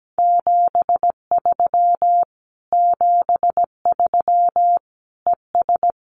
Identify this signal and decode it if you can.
Morse code, heard as a keyed tone, about 17 wpm, 700 Hz. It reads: 73 73 ES
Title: Common QSO Phrases Encore 17wpm